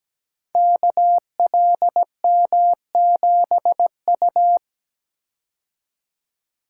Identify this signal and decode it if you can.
Morse code, heard as a keyed tone, about 17 wpm, 700 Hz. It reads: KLM7U
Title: Code Groups - 5 Letter:Numbers - 17wpm